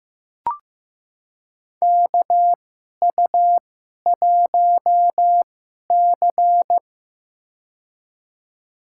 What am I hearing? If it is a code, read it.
Morse code, heard as a keyed tone, about 15 wpm, spaced as for 11 wpm, 700 Hz. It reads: KU1C
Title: USA Call Signs - 4 Characters - 2x Character Spacing - 15wpm